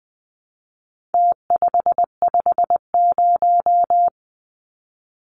Morse code, heard as a keyed tone, about 20 wpm, 700 Hz. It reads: T550
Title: Numbers - Contest Style - 4-Digit 20wpm